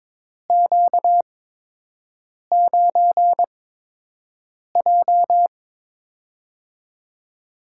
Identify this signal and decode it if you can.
Morse code, heard as a keyed tone, about 22 wpm, spaced as for 6 wpm, 700 Hz. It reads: Q9J
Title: USA Call Signs - 4 Characters - 8x Character Spacing - 22wpm